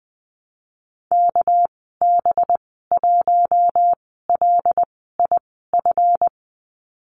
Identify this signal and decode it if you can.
Morse code, heard as a keyed tone, about 19 wpm, spaced as for 14 wpm, 700 Hz. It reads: KB1LIF
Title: 200 Lesson 32g - Letter k - Call Signs 20wpm